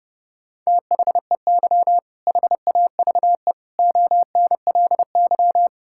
Morse code, heard as a keyed tone, about 30 wpm, 700 Hz. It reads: THEY HAVE ONLY
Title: Sentences from Top 1600 Words 30wpm